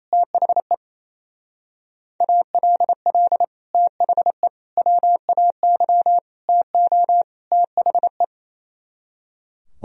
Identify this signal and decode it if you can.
Morse code, heard as a keyed tone, about 32 wpm, 700 Hz. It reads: THE ALL THE WAY TO THE
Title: Speed-Racing - Sets of 5 Words 28wpm